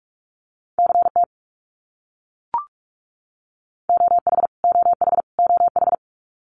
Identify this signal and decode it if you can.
Morse code, heard as a keyed tone, about 44 wpm, 700 Hz. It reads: CT OH OH OH
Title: US States Repeated 3x 45wpm